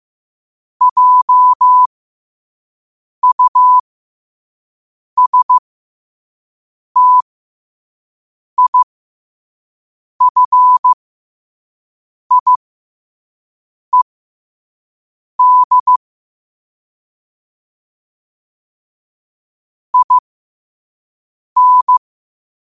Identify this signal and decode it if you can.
Morse code, heard as a keyed tone, about 15 wpm, spaced as for 5 wpm, 1000 Hz. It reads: JUSTIFIED IN
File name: qotdfiles / 5wpm